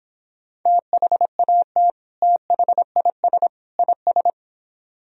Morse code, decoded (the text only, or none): THAT THIS IS